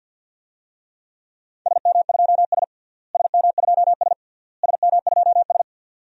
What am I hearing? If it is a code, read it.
Morse code, heard as a keyed tone, about 46 wpm, 700 Hz. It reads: SM2S SM2S SM2S
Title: DXCC-100 Call Signs - 4 Characters - Repeated 3x - 50wpm